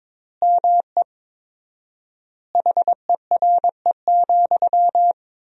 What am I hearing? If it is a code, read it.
Morse code, heard as a keyed tone, about 22 wpm, 700 Hz. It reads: ME HERE,
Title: 4x Word-Spacing - Sentences from Top 500 Words - 22wpm